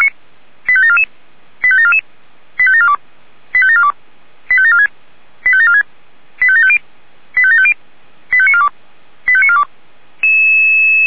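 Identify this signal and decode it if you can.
Five-tone selective call (ZVEI), heard as quick runs of five tones, tones of about 70 ms. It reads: 85744 85744 85742 85742 85746 85746 85750 85750 85942 85942
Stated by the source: SDR receiver recording of ZVEI five-tone selective call, code 85758, for Feuerwehr Orlamünde fire brigade.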